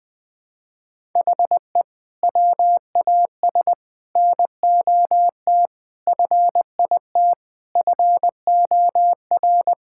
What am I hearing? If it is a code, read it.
Morse code, heard as a keyed tone, about 21 wpm, 700 Hz. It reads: HE WAS NOT FIT FOR
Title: Sentences from Top 1000 Words - Review Part 5 - 20wpm